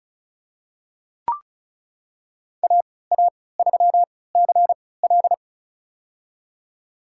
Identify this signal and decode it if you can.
Morse code, heard as a keyed tone, about 36 wpm, spaced as for 20 wpm, 690 Hz. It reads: AA3CL
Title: USA Call Signs - 5 Characters - 3x Character Spacing - 35wpm